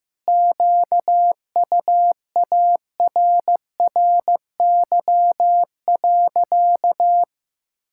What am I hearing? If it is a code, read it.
Morse code, heard as a keyed tone, about 15 wpm, 700 Hz. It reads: QUARRY.
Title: Pangram Challenge v2 - 15wpm